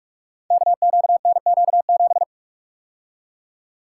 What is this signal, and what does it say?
Morse code, heard as a keyed tone, about 44 wpm, 700 Hz. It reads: KQNQ7
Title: Code Groups - 5 Letter:Numbers - 45wpm